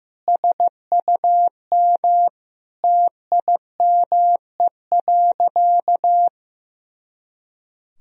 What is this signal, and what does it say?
Morse code, heard as a keyed tone, about 15 wpm, 700 Hz. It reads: SUM TIME.
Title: Sentences from Top 300 Words and Abbreviations 15wpm